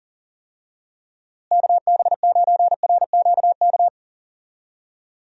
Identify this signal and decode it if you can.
Morse code, heard as a keyed tone, about 38 wpm, 700 Hz. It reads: KB9RQK
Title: USA Call Signs - 6 Characters - Repeated 2x - 40wpm